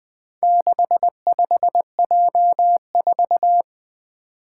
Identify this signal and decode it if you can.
Morse code, heard as a keyed tone, about 21 wpm, 700 Hz. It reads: 65J4
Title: Four Challenging Letter-Number Combinations 20wpm